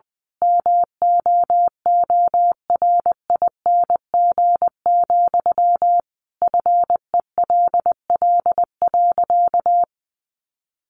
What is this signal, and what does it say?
Morse code, heard as a keyed tone, about 20 wpm, 700 Hz. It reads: MOORING, FELL.
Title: A Sound Of Thunder 20wpm